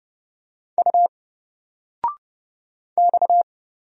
Morse code, heard as a keyed tone, about 30 wpm, 700 Hz. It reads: U X
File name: Single-Letter-Number-30wpm